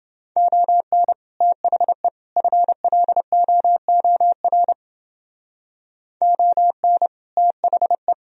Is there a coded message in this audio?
Yes: Morse code from 0.4 to 8.1 s, heard as a keyed tone, about 30 wpm, 700 Hz. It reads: ON THE FLOOR ON THE